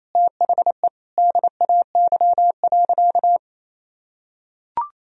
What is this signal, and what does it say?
Morse code, heard as a keyed tone, about 27 wpm, 700 Hz. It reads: THE DAY.
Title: Sentences from Top 1000 Words - Review Part 2 - 28wpm